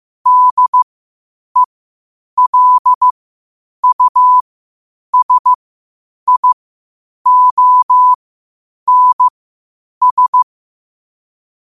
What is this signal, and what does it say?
Morse code, heard as a keyed tone, about 15 wpm, spaced as for 8 wpm, 1000 Hz. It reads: DELUSIONS
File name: qotdfiles / 8wpm